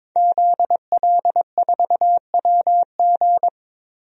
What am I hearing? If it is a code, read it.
Morse code, heard as a keyed tone, about 21 wpm, 700 Hz. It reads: ZL4WG
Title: DXCC-100 Call Signs - 5 Characters - Repeated 3x - 22wpm